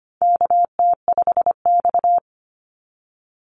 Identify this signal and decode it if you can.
Morse code, heard as a keyed tone, about 25 wpm, 700 Hz. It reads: KT5X
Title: NA SOTA Chaser Callsigns 25wpm